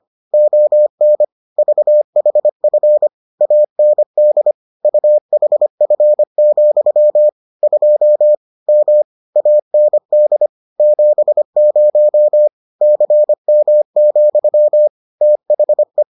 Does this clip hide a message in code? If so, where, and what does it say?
Yes: Morse code from 0.3 to 16.0 s, heard as a keyed tone, about 25 wpm, 600 Hz. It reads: ON VHF AND UHF, 2 M AND 70 CM, THE